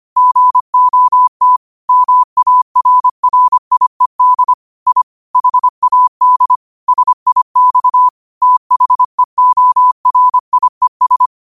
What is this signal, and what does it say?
Morse code, heard as a keyed tone, about 25 wpm, 1000 Hz. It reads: GOT MARRIED I HAD SIX THEORIES
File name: qotdfiles / 25wpm